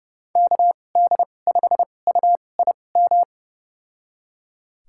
Morse code, heard as a keyed tone, about 30 wpm, spaced as for 22 wpm, 700 Hz. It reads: KD5UIM
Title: USA Call Signs - 2x Character Spacing - 30wpm